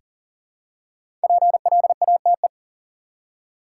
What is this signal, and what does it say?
Morse code, heard as a keyed tone, about 40 wpm, 700 Hz. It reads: PLATE
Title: Five Letter Words - 40wpm